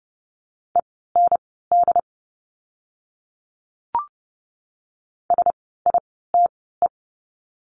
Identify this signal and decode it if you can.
Morse code, heard as a keyed tone, about 30 wpm, spaced as for 17 wpm, 700 Hz. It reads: END SITE
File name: Lesson 16f - Letter c - Words 30wpm